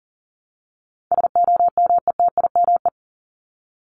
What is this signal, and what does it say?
Morse code, heard as a keyed tone, about 40 wpm, 700 Hz. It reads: SOMETIME
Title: ES:S Suffix - Top 100 Words with Base Words 40wpm